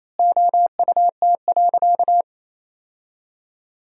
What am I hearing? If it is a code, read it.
Morse code, heard as a keyed tone, about 28 wpm, 700 Hz. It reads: OUT.
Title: 4x Word-Spacing - Sentences from Top 500 Words - 28wpm